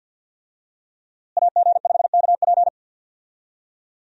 Morse code, heard as a keyed tone, about 50 wpm, 700 Hz. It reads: AG5KP
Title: USA Call Signs - 5 Characters - Repeated 2x - 50wpm